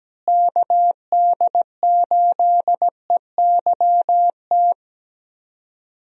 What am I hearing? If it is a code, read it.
Morse code, heard as a keyed tone, about 17 wpm, 700 Hz. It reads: KD8EYT
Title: USA Call Signs - 6 Characters - Repeated 3x - 17wpm